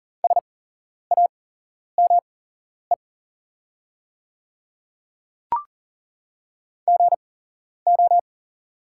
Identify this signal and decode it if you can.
Morse code, heard as a keyed tone, about 40 wpm, spaced as for 11 wpm, 690 Hz. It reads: SAME GO
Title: 8x Character Spacing - Top 300 Words - 40wpm